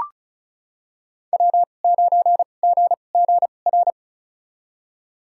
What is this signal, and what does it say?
Morse code, heard as a keyed tone, about 36 wpm, spaced as for 25 wpm, 700 Hz. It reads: W9GGR
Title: USA Call Signs - 5 Characters - 2x Character Spacing - 35wpm